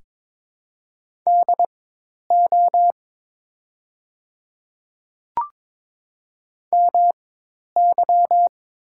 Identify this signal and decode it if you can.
Morse code, heard as a keyed tone, about 23 wpm, spaced as for 10 wpm, 700 Hz. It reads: DO MY
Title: 4x Character Spacing - Top 100 Words - 22wpm